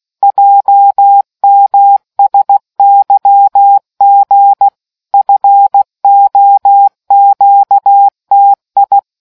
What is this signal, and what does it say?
Morse code, heard as a keyed tone, about 16 wpm, 800 Hz. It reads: JMSYG FOQTI